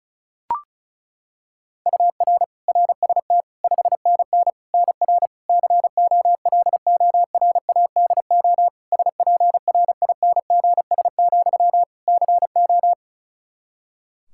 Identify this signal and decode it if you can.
Morse code, heard as a keyed tone, about 34 wpm, 700 Hz. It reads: UR RST 5NN NR COLORADO SPRINGS, CO
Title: Practice QSO - Exchange 1 - Beginning - 35wpm